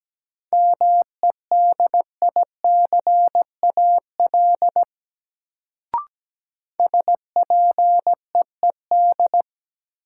Morse code, heard as a keyed tone, about 17 wpm, 700 Hz. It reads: MEDICAL SPEED
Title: Sentences from Top 800 Words - New Word Drill - 17wpm